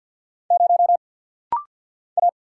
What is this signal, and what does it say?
Morse code, heard as a keyed tone, about 50 wpm, 690 Hz. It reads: 0 A
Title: Single Letter-Number - Rapid-Fire 50wpm